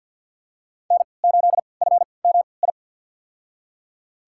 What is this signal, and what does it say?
Morse code, heard as a keyed tone, about 52 wpm, spaced as for 29 wpm, 690 Hz. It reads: N8FMI
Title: USA Call Signs - 5 Characters - 3x Character Spacing - 50wpm